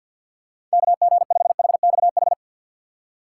Morse code, heard as a keyed tone, about 52 wpm, 700 Hz. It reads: KG5HXH
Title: USA Call Signs - 6 Characters - 6x Character Spacing - 50wpm